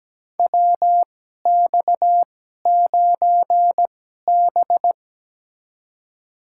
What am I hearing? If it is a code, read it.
Morse code, heard as a keyed tone, about 17 wpm, spaced as for 12 wpm, 700 Hz. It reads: WX9B
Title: USA Call Signs - 2x Character Spacing - 17wpm